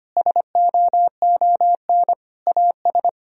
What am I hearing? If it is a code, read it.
Morse code, heard as a keyed tone, about 25 wpm, 700 Hz. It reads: SOON AS
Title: Sentences from Top 1000 Words - Review Part 2 - 25wpm